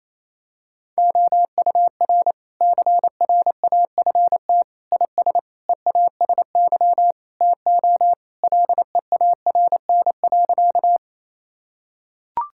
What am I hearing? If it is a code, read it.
Morse code, heard as a keyed tone, about 28 wpm, 700 Hz. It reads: OUR CRAFT IS EASY TO LEARN.
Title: Sentences from Top 1000 Words - Review Part 6 - 28wpm